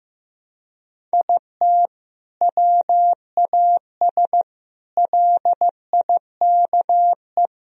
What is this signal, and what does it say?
Morse code, heard as a keyed tone, about 16 wpm, 700 Hz. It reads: IT WAS LIKE
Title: Sentences from Top 1000 Words - Review Part 1 - 15wpm